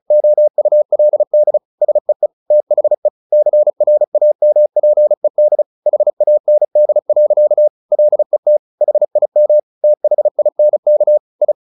Morse code, heard as a keyed tone, about 34 wpm, 600 Hz. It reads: OULD SEE THE CRAMPED HAND. LET HIM THINK I
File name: the old man and the sea / the old man and the sea 1 35wpm